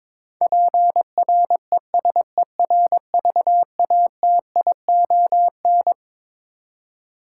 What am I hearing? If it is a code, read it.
Morse code, heard as a keyed tone, about 22 wpm, 700 Hz. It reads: PRESERVATION
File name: Twelve Letter Words - 22wpm